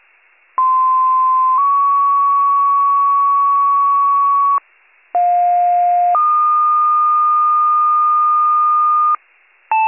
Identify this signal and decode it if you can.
Two-tone page, heard as a one-second tone, then a three-second tone, about 1040 then 1120 Hz, about 710 then 1220 Hz. It is heard from 0.6 to 9.2 s.